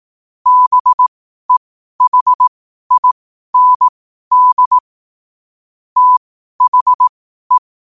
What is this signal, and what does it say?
Morse code, heard as a keyed tone, about 18 wpm, spaced as for 12 wpm, 1000 Hz. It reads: BEHIND THE
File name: qotdfiles / 12wpm